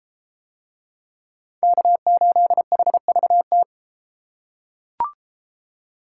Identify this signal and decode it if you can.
Morse code, heard as a keyed tone, about 31 wpm, 700 Hz. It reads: K8HVT